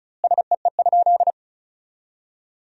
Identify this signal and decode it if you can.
Morse code, heard as a keyed tone, about 34 wpm, 700 Hz. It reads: SEE?
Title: 2x Word-Spacing - Sentences from Top 100 Words - 35wpm